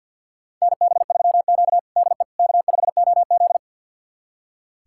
Morse code, heard as a keyed tone, about 47 wpm, 700 Hz. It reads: NB3Q DE K5OZ
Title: USA Callsign DE Callsign - 4 Characters - 50wpm